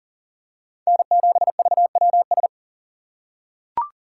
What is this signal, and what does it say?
Morse code, heard as a keyed tone, about 40 wpm, 700 Hz. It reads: N7VWS
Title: USA Call Signs - 5 Characters - 6x Character Spacing - 40wpm